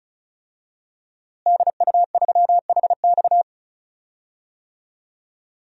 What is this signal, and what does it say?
Morse code, heard as a keyed tone, about 36 wpm, 700 Hz. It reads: DU3HX
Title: DXCC-100 Call Signs - 5 Characters - Repeated 2x - 35wpm